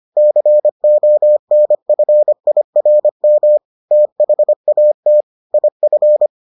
Morse code, heard as a keyed tone, about 25 wpm, 600 Hz. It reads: CONFIRM THAT IF